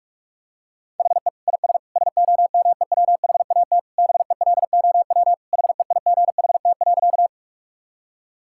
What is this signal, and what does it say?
Morse code, heard as a keyed tone, about 41 wpm, 700 Hz. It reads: HE IS SOMEWHAT BELOW HEIGHT.
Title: Sentences from Top 1600 Words 45wpm